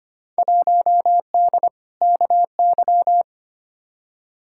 Morse code, heard as a keyed tone, about 25 wpm, 700 Hz. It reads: 1D KY
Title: Field Day Exchanges - 25wpm